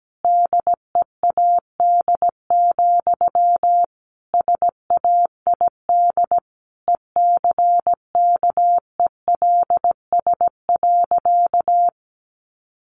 Morse code, heard as a keyed tone, about 17 wpm, 700 Hz. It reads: DEAD, SAID ECKELS.
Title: A Sound Of Thunder 17wpm